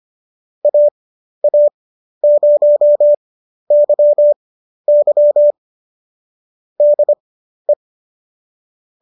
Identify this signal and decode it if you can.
Morse code, heard as a keyed tone, about 25 wpm, spaced as for 12 wpm, 590 Hz. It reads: AA0YY DE